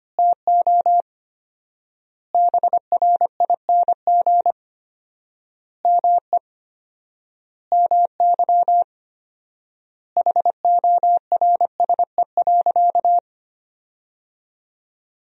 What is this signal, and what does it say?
Morse code, heard as a keyed tone, about 25 wpm, 700 Hz. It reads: TO BRING ME MY HORSE.
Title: 4x Word-Spacing - Sentences from Top 500 Words - 25wpm